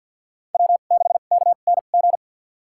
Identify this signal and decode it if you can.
Morse code, heard as a keyed tone, about 50 wpm, spaced as for 36 wpm, 700 Hz. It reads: W6KNG
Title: USA Call Signs - 5 Characters - 2x Character Spacing - 50wpm